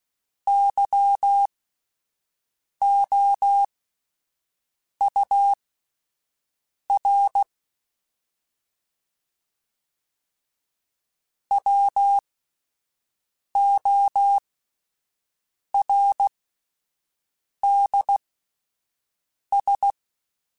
Morse code, heard as a keyed tone, about 16 wpm, spaced as for 5 wpm, 800 Hz. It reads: YOUR WORDS
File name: Spaced / Chuckles Spaced-mp3s / SoftWords sp @10